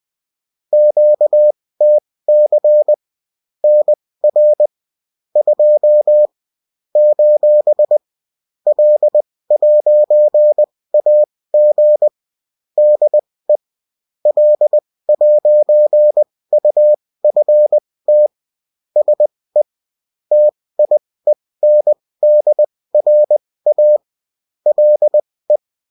Morse code, heard as a keyed tone, about 20 wpm, spaced as for 16 wpm, 600 Hz. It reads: QTC NR 2 : L'AG DE L'UFT SE TIENDRA LE
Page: bulletin-uft-sous-forme-texte-et-en-cw